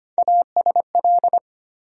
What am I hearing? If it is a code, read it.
Morse code, heard as a keyed tone, about 25 wpm, 700 Hz. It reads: ASL
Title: USA Call Sign Postfixes - 3 Letters 25wpm